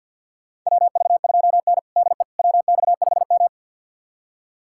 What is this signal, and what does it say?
Morse code, heard as a keyed tone, about 48 wpm, 700 Hz. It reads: WV2N DE WX5M
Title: USA Callsign DE Callsign - 4 Characters - 50wpm